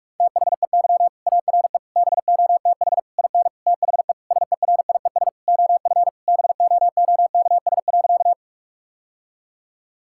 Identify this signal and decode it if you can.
Morse code, heard as a keyed tone, about 42 wpm, 700 Hz. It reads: THEY ARE BOTH IN THE SERIES OF BOOKS.
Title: Sentences from Top 1000 Words - Review Part 5 - 45wpm